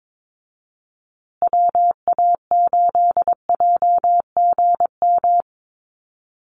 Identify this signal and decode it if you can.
Morse code, heard as a keyed tone, about 22 wpm, 700 Hz. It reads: WA8JGM
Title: USA Call Signs Repeated 2x 22wpm